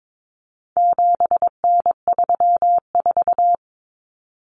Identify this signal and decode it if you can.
Morse code, heard as a keyed tone, about 22 wpm, 700 Hz. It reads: 7N34
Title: Common Cut Numbers - 4-Digit 22wpm